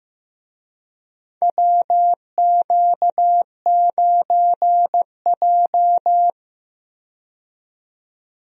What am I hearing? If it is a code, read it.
Morse code, heard as a keyed tone, about 15 wpm, 700 Hz. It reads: WQ9J
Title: USA Call Signs - 4 Characters - 15wpm